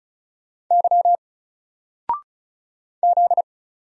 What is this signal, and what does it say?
Morse code, heard as a keyed tone, about 34 wpm, 700 Hz. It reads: Y Z
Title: Single Letter-Number 35wpm